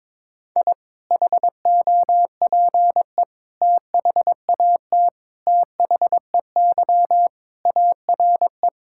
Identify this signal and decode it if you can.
Morse code, heard as a keyed tone, about 22 wpm, 700 Hz. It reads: I HOPE THAT THEY ARE
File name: practice / 22wpm / Sentences with 3-Word Phrases - 22wpm